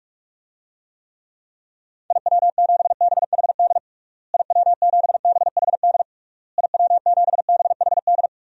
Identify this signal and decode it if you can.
Morse code, heard as a keyed tone, about 41 wpm, 700 Hz. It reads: IW7BHD IW7BHD IW7BHD
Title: DXCC-100 Call Signs - 6 Characters - Repeated 3x - 45wpm